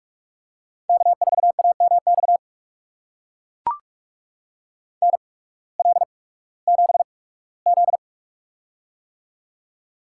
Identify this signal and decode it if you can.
Morse code, heard as a keyed tone, about 43 wpm, spaced as for 21 wpm, 700 Hz. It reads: K4AMX NL7Z
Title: USA Call Signs - 8x Character Spacing - 45wpm